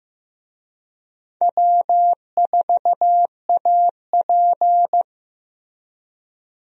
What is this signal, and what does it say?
Morse code, heard as a keyed tone, about 15 wpm, 700 Hz. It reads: W4AP
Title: USA Call Signs - 4 Characters - Repeated 3x - 15wpm